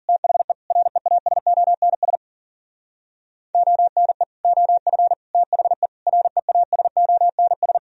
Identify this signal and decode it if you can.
Morse code, heard as a keyed tone, about 51 wpm, 700 Hz. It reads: THE REASONS ONE OF THE REASONS